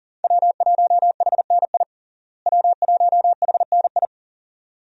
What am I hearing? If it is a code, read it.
Morse code, heard as a keyed tone, about 37 wpm, 700 Hz. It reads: W1HNI W1HNI
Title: USA Call Signs - 5 Characters - Repeated 2x - 40wpm